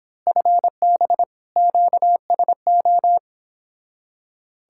Morse code, heard as a keyed tone, about 26 wpm, 700 Hz. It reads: FB QSO
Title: Speed-Racing Common QSO Phrases - 22wpm